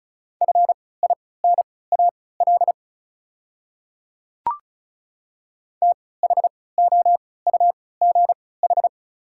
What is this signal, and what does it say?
Morse code, heard as a keyed tone, about 34 wpm, spaced as for 20 wpm, 690 Hz. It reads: FINAL THOUGH
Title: 3x Character Spacing - Top 500 Words - 35wpm